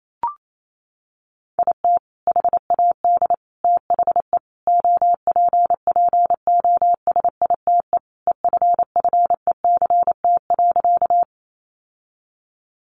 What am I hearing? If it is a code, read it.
Morse code, heard as a keyed tone, about 28 wpm, 700 Hz. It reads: IT HAD THE OPPOSITE EFFECT.
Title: Sentences from Top 1300 Words 28wpm